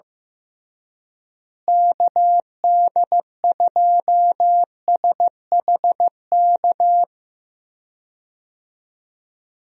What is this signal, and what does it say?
Morse code, heard as a keyed tone, about 15 wpm, 700 Hz. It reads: KD2SHK
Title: USA Call Signs - 6 Characters - Repeated 3x - 15wpm